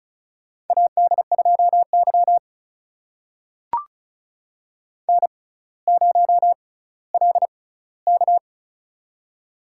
Morse code, heard as a keyed tone, about 36 wpm, spaced as for 18 wpm, 700 Hz. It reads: AD2Y N0LK
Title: USA Call Signs - 4 Characters - 6x Character Spacing - 35wpm